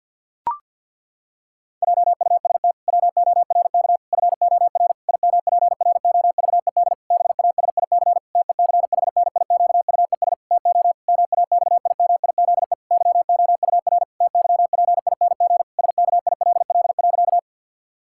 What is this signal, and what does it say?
Morse code, heard as a keyed tone, about 50 wpm, 700 Hz. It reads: JUST WORK FOR IMPROVED BASIC TECHNIQUES TO MAXIMIZE YOUR TYPING SKILL.
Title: Pangram Challenge v2 - 50wpm